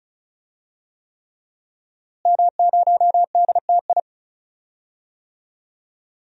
Morse code, heard as a keyed tone, about 34 wpm, 700 Hz. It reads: M0DTI
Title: DXCC-100 Call Signs - 5 Characters - Repeated 3x - 35wpm